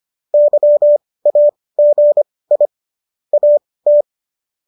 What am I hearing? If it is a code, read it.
Morse code, heard as a keyed tone, about 25 wpm, spaced as for 18 wpm, 590 Hz. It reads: YAGI AT